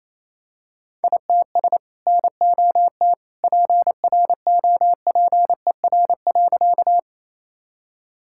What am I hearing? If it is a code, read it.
Morse code, heard as a keyed tone, about 28 wpm, 700 Hz. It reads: ITS NOT PROPER.
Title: Sentences from Top 1000 Words - Review Part 5 - 28wpm